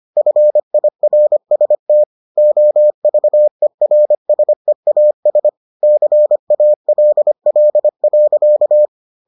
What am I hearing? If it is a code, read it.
Morse code, heard as a keyed tone, about 25 wpm, 600 Hz. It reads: FIRST OVERSEAS CALL.